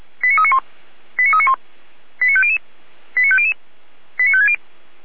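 Five-tone selective call (ZVEI), heard as quick runs of five tones, tones of about 70 ms. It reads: 89391 89391 89500 89500 89560
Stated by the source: ZVEI five-tone selective call received on SDR.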